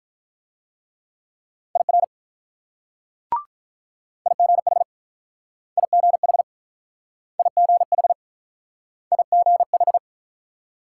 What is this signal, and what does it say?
Morse code, heard as a keyed tone, about 44 wpm, 700 Hz. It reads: IR IGH IGH IGH IGH
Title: Speed-Racing Multi-letter Phonemes - 35wpm